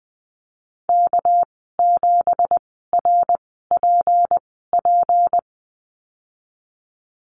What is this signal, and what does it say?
Morse code, heard as a keyed tone, about 20 wpm, spaced as for 15 wpm, 700 Hz. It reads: K7RPP
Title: 231 Lesson 37g - Letter x - Call Signs 20wpm